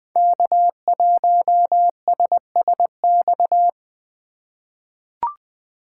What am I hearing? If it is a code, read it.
Morse code, heard as a keyed tone, about 20 wpm, 700 Hz. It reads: K1SSX
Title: Code Groups - 5 Letter:Numbers - 20wpm